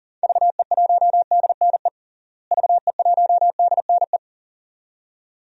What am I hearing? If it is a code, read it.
Morse code, heard as a keyed tone, about 40 wpm, 700 Hz. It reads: VE1DNE VE1DNE
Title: DXCC-100 Call Signs - 6 Characters - Repeated 2x - 40wpm